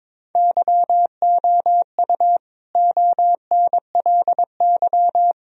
Morse code, heard as a keyed tone, about 22 wpm, 700 Hz. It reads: YOU ONLY
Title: Sentences from Top 1000 Words - Review Part 2 - 22wpm